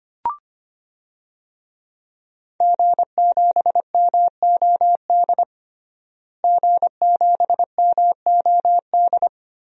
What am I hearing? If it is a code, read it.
Morse code, heard as a keyed tone, about 25 wpm, 700 Hz. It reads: G7MOD G7MOD
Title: DXCC-100 Call Signs - 5 Characters - Repeated 3x - 25wpm